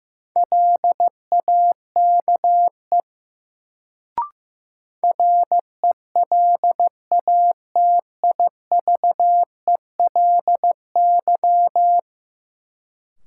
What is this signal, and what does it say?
Morse code, heard as a keyed tone, about 15 wpm, 700 Hz. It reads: LAKE RELATIVELY
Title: Sentences from Top 1100 Words - New Word Drill Alt - 15wpm